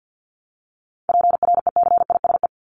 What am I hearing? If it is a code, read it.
Morse code, heard as a keyed tone, about 50 wpm, 700 Hz. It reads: PRECISE
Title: LY Suffix - Top 100 Words with Base Words 50wpm